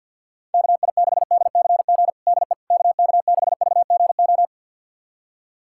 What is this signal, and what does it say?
Morse code, heard as a keyed tone, about 47 wpm, 700 Hz. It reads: KI6DCG DE KK6VGO
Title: USA Callsign DE Callsign - 6 Characters - 50wpm